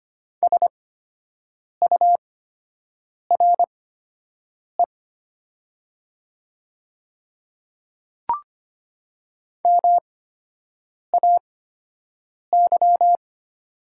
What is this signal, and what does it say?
Morse code, heard as a keyed tone, about 25 wpm, spaced as for 7 wpm, 700 Hz. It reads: SURE MAY
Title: 8x Character Spacing - Top 300 Words - 25wpm